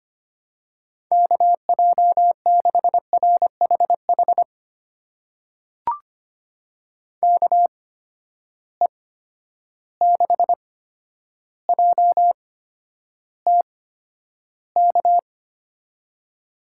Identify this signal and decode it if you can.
Morse code, heard as a keyed tone, about 25 wpm, spaced as for 11 wpm, 700 Hz. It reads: KJ6RHH KE6JTK
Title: USA Call Signs - 6 Characters - 8x Character Spacing - 25wpm